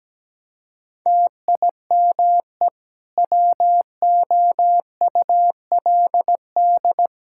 Morse code, heard as a keyed tone, about 17 wpm, 700 Hz. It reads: TIME WOULD
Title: Sentences from Top 1000 Words - Review Part 5 - 17wpm